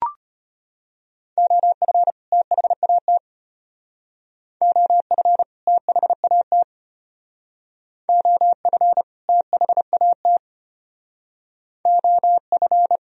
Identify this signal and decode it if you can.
Morse code, heard as a keyed tone, about 32 wpm, 700 Hz. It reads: OF THAT OF THAT OF THAT OF